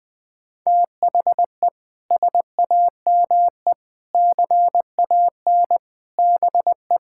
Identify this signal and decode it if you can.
Morse code, heard as a keyed tone, about 21 wpm, 700 Hz. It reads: THE SAME CAN BE
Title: Sets of 6 Words - 20wpm